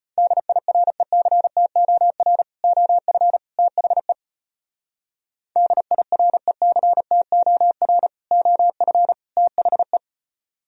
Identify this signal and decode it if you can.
Morse code, heard as a keyed tone, about 29 wpm, 700 Hz. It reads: DIRECTOR OF THE DIRECTOR OF THE
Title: Speed-Racing - Sets of 3 Words 25wpm